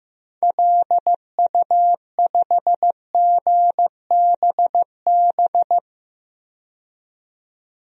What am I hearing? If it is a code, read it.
Morse code, heard as a keyed tone, about 15 wpm, 700 Hz. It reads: LU5GBB
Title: DXCC-100 Call Signs - 6 Characters - 15wpm